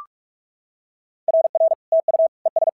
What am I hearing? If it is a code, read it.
Morse code, heard as a keyed tone, about 46 wpm, 650 Hz. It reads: RR TU ES
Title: SOTA QSO Practice 45wpm